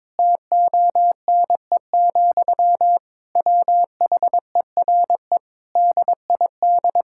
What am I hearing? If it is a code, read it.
Morse code, heard as a keyed tone, about 21 wpm, 700 Hz. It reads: TONE, WHERE DID
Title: Sentences from Top 1600 Words 22wpm